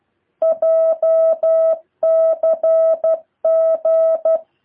Morse code, heard as a keyed tone, about 12 wpm, 650 Hz. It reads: JCG
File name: jh8cbh / 2023cw